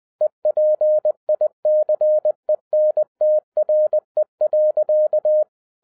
Morse code, heard as a keyed tone, about 20 wpm, 610 Hz. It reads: EPICENTRE.